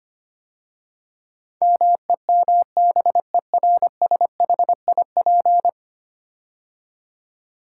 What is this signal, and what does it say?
Morse code, heard as a keyed tone, about 25 wpm, 700 Hz. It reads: MEMBERSHIP
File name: 10 Letter Words - First 3 Letters Given - 25wpm